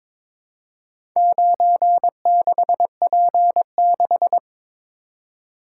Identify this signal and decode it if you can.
Morse code, heard as a keyed tone, about 22 wpm, 700 Hz. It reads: 96P6
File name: practice / 22wpm / Four Letter-Number Combinations - 22wpm